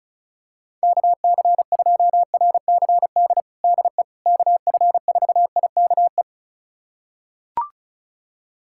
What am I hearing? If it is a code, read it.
Morse code, heard as a keyed tone, about 36 wpm, 700 Hz. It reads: KC2RCD DE KF4IKE
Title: USA Callsign DE Callsign 35wpm